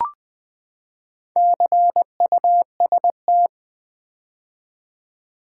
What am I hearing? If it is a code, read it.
Morse code, heard as a keyed tone, about 20 wpm, 700 Hz. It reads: CUST
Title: Build A Word - 4 Letters At A Time - 8-Letter Words - 20wpm